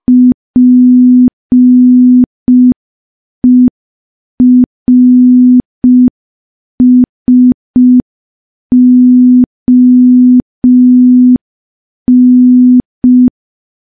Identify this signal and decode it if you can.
Morse code, heard as a keyed tone, about 5 wpm, 250 Hz. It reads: PERSON
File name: morse / morse-training-05WPM